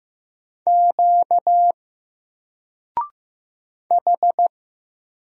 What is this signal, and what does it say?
Morse code, heard as a keyed tone, about 15 wpm, 700 Hz. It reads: Q H
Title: Single Letter-Number Alt 15wpm